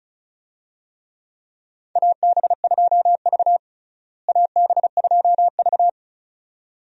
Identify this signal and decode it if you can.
Morse code, heard as a keyed tone, about 36 wpm, 700 Hz. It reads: AB2V AB2V
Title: USA Call Signs - 4 Characters - Repeated 2x - 35wpm